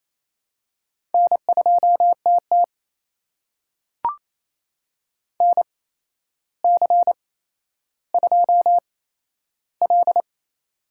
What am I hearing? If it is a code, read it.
Morse code, heard as a keyed tone, about 28 wpm, spaced as for 12 wpm, 700 Hz. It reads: N2TT NC2L
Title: USA Call Signs - 4 Characters - 8x Character Spacing - 28wpm